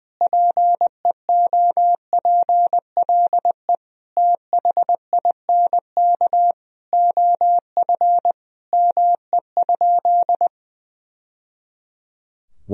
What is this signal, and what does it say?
Morse code, heard as a keyed tone, about 20 wpm, 700 Hz. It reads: PEOPLE THINK OF ME?
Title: Sentences from Top 1000 Words - Review Part 3 - 20wpm